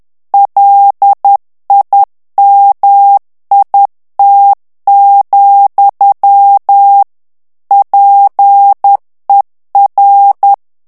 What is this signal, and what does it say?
Morse code, heard as a keyed tone, about 10 wpm, 800 Hz. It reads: LIMIT, PER
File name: ANGLAIS texte clair 3 vitesse 10 mots minute